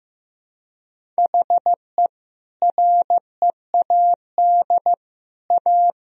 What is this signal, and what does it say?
Morse code, heard as a keyed tone, about 15 wpm, 700 Hz. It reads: HE READ A
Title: Sentences from Top 1000 Words - Review Part 1 - 15wpm